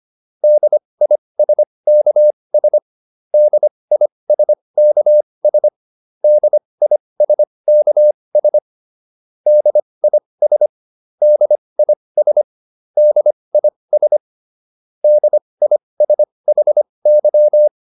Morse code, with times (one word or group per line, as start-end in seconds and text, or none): 0.4-2.8 DISKS
3.3-5.7 DISKS
6.2-8.6 DISKS
9.5-10.7 DIS
11.2-12.4 DIS
13.0-14.2 DIS
15.0-17.7 DISHY